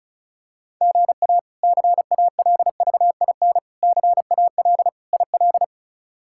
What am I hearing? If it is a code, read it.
Morse code, heard as a keyed tone, about 34 wpm, 700 Hz. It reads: GA CALVIN CAL IL
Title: K1USN SST QSO Practice - 35wpm